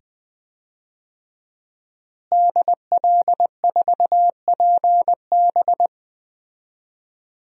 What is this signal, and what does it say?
Morse code, heard as a keyed tone, about 20 wpm, 700 Hz. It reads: DL4PB
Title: DXCC-100 Call Signs - 5 Characters - Repeated 2x - 20wpm